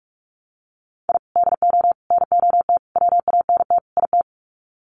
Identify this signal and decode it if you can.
Morse code, heard as a keyed tone, about 44 wpm, 690 Hz. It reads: I DO NOT WANT IT
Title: Sets of 5 Words Encore2 45wpm